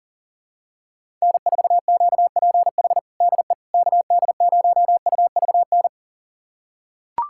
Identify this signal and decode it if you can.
Morse code, heard as a keyed tone, about 37 wpm, 700 Hz. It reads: N4QPH DE KD0UVN
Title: USA Callsign DE Callsign 40wpm